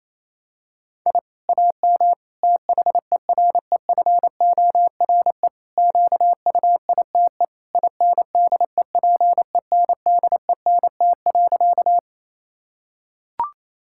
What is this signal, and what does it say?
Morse code, heard as a keyed tone, about 28 wpm, 700 Hz. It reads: I AM THEREFORE QUITE INDEPENDENT.
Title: Sentences from Top 1000 Words - Review Part 6 - 28wpm